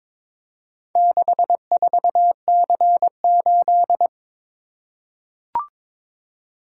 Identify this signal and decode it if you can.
Morse code, heard as a keyed tone, about 22 wpm, 700 Hz. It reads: 64C8
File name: practice / 22wpm / Four Letter-Number Combinations - 22wpm